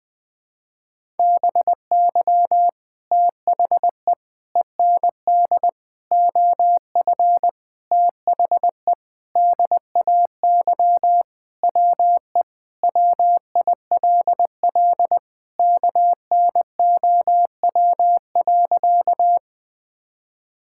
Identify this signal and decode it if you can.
Morse code, heard as a keyed tone, about 20 wpm, 700 Hz. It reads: BY THE END OF THE DAY WE WILL KNOW.
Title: Sentences with 4-Word Phrases - Spoken Beginning - 20wpm